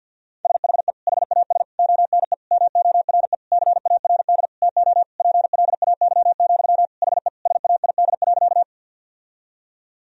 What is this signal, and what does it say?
Morse code, heard as a keyed tone, about 50 wpm, 700 Hz. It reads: SHE HAS ONE MORE CARD TO PLAY, HE SAID.
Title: Sentences from Top 1000 Words - Review Part 4 - 50wpm